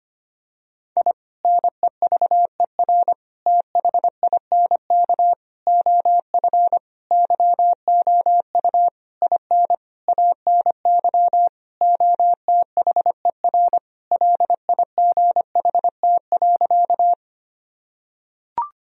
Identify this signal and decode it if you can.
Morse code, heard as a keyed tone, about 25 wpm, 700 Hz. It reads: I NEVER THINK OF YOU IN ANY OTHER LIGHT.
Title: Sentences from Top 1000 Words - Review Part 1 - 25wpm